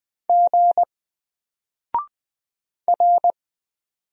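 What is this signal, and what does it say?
Morse code, heard as a keyed tone, about 20 wpm, 700 Hz. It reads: G R
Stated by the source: Single letter and number Morse drill at 20 wpm, from N4LCD.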